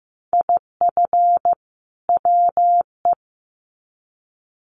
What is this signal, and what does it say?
Morse code, heard as a keyed tone, about 15 wpm, 700 Hz. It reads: IF WE
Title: Sets of 2 Words 15wpm